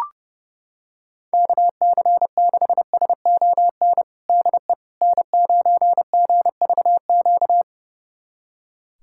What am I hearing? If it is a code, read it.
Morse code, heard as a keyed tone, about 32 wpm, 700 Hz. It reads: KC6SON DE N9GVQ
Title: USA Callsign DE Callsign 30wpm